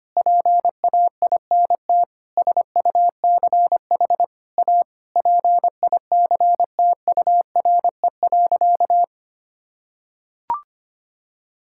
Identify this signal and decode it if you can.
Morse code, heard as a keyed tone, about 25 wpm, 700 Hz. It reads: PAINT SUCH A PICTURE.
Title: Sentences from Top 1000 Words - Review Part 6 - 25wpm